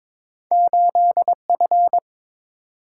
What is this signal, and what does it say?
Morse code, heard as a keyed tone, about 22 wpm, 700 Hz. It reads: 8F
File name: practice / 22wpm / Two Letter-Number Combinations - 22wpm